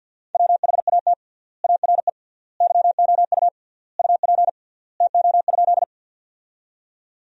Morse code, heard as a keyed tone, about 50 wpm, 700 Hz. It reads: WHAT ARE YOU UP TO?
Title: 3x Word-Spacing - Sentences from Top 100 Words - 50wpm